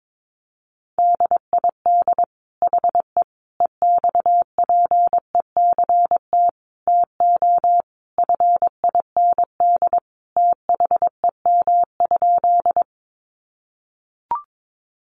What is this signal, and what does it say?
Morse code, heard as a keyed tone, about 22 wpm, 700 Hz. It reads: DID HE EXPECT TO FIND THEM?
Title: Sentences from Top 1100 Words 22wpm